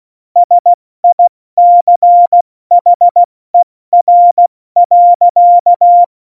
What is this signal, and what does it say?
Morse code, heard as a keyed tone, about 16 wpm, 700 Hz. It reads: SICHER.
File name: Morsekurs / Witze / Witz 26. Januar 2021, 70 F80